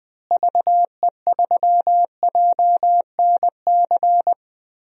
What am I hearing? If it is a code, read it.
Morse code, heard as a keyed tone, about 21 wpm, 700 Hz. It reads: VE3JNC
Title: DXCC-100 Call Signs - 6 Characters - Repeated 2x - 20wpm